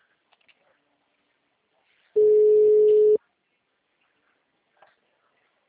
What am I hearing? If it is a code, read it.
Ringback tone (Europe, 425 Hz).
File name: out- -2009